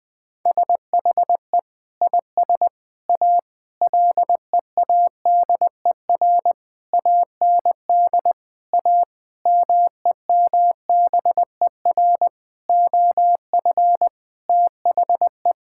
Morse code, heard as a keyed tone, about 20 wpm, 700 Hz. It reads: SHE IS A LEADER AND A MEMBER OF THE
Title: Sentences with 5-Word Phrases - 20wpm